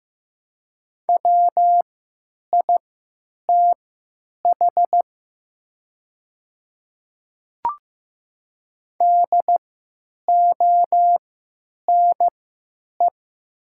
Morse code, heard as a keyed tone, about 15 wpm, spaced as for 9 wpm, 700 Hz. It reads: WITH DONE
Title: 3x Character Spacing - Top 500 Words - 15wpm